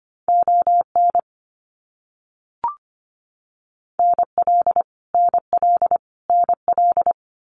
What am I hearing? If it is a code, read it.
Morse code, heard as a keyed tone, about 25 wpm, 700 Hz. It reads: ON NL NL NL
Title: Canadian Provinces and Territories - Repeated 3x - 25wpm